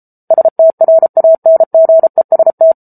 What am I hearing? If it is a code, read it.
Morse code, heard as a keyed tone, about 34 wpm, 650 Hz. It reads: STRANGEST